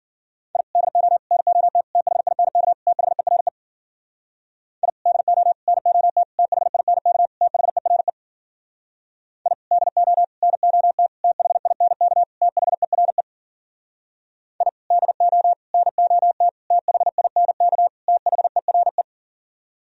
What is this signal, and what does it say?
Morse code, heard as a keyed tone, about 51 wpm, 700 Hz. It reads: I DO NOT THINK THERE I DO NOT THINK THERE I DO NOT THINK THERE I DO NOT THINK THERE